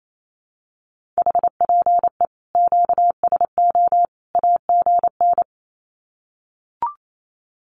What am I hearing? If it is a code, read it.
Morse code, heard as a keyed tone, about 28 wpm, 700 Hz. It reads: HPE QSO AGN
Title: Common QSO Phrases 28wpm